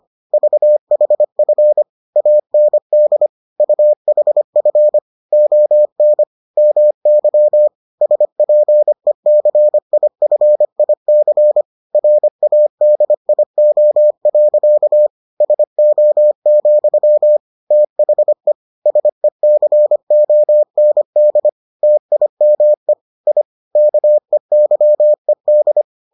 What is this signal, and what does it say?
Morse code, heard as a keyed tone, about 25 wpm, 600 Hz. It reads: VHF AND UHF ON MY SPECIFIC RADIO. SO, THE SECOND TIME I KEYED